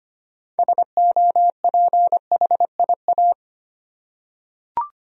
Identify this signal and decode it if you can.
Morse code, heard as a keyed tone, about 25 wpm, 700 Hz. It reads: SOPHIA
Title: USA Common Names - 25wpm